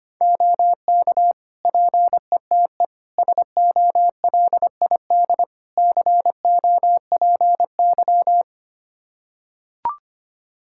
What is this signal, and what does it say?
Morse code, heard as a keyed tone, about 25 wpm, 700 Hz. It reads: OK PETE SOLID COPY
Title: Practice QSO - Exchange 3 - More Info - 25wpm